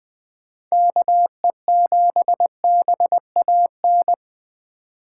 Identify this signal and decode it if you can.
Morse code, heard as a keyed tone, about 20 wpm, 700 Hz. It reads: KE7BAN
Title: Speed Racing - USA 2x3 Call Signs 17wpm